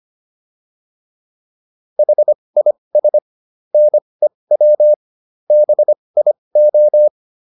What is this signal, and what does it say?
Morse code, heard as a keyed tone, about 25 wpm, spaced as for 20 wpm, 600 Hz. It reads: HIS NEW BIO